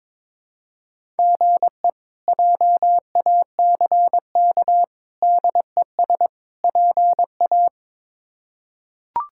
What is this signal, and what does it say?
Morse code, heard as a keyed tone, about 22 wpm, 700 Hz. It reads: GE JACK DES PA
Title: K1USN SST QSO Practice - 22wpm